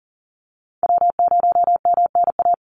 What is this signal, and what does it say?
Morse code, heard as a keyed tone, about 40 wpm, 700 Hz. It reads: W0MNA
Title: NA SOTA Chaser Callsigns 40wpm